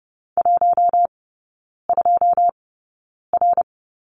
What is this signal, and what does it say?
Morse code, heard as a keyed tone, about 30 wpm, spaced as for 9 wpm, 700 Hz. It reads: 12R
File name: course / 30:10wpm / 087 Lesson 15e - Number 5 - Three Characters 30wpm